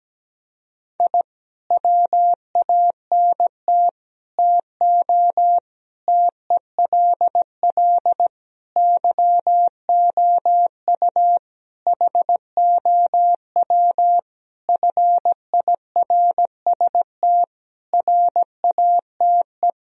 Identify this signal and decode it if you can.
Morse code, heard as a keyed tone, about 17 wpm, 700 Hz. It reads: I WANT TO TELL YOU HOW FIRST RATE
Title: Sentences from Top 1000 Words - Review Part 4 - 17wpm